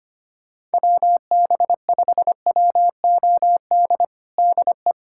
Morse code, heard as a keyed tone, about 25 wpm, 700 Hz. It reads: WB5WOD DE
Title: USA Callsign DE Callsign - 6 Characters - 25wpm